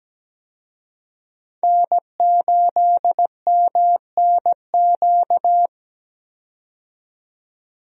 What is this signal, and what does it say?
Morse code, heard as a keyed tone, about 17 wpm, 700 Hz. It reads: N8MNQ
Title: USA Call Signs - 5 Characters - 17wpm